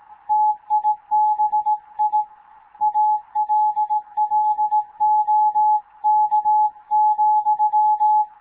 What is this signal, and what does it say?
Morse code, heard as a keyed tone, about 18 wpm, 850 Hz. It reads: TIBI ALLOK,
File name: Morze-HA2MN / hangp / 2 11 GYAKORLAT